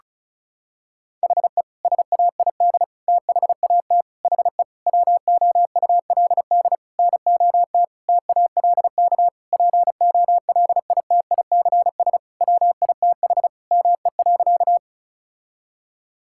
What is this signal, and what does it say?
Morse code, heard as a keyed tone, about 34 wpm, 700 Hz. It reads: HE SAID THAT HE WOULD NOT TALK POLITICS WITH ME.
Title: Sentences from Top 1000 Words - Review Part 6 - 35wpm